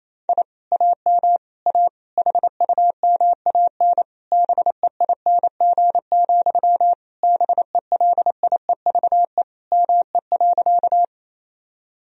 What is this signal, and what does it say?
Morse code, heard as a keyed tone, about 28 wpm, 700 Hz. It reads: I AM A HUMAN BEING, BELIEVE ME.
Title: Sentences from Top 1000 Words - Review Part 3 - 28wpm